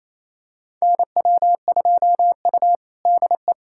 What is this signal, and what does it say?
Morse code, heard as a keyed tone, about 28 wpm, 700 Hz. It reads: NW2U DE
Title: USA Callsign DE Callsign 28wpm